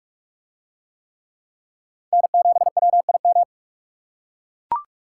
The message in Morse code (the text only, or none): N7WIM